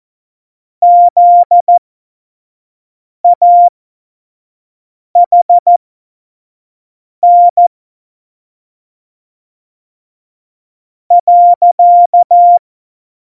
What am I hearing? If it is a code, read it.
Morse code, heard as a keyed tone, about 14 wpm, spaced as for 4 wpm, 700 Hz. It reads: ZAHN.